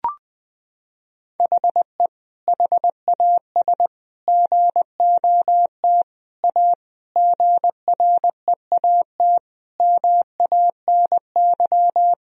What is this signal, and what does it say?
Morse code, heard as a keyed tone, about 20 wpm, 700 Hz. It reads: HE HAS GOT A GREAT MANY
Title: Sentences from Top 1000 Words - Review Part 1 - 20wpm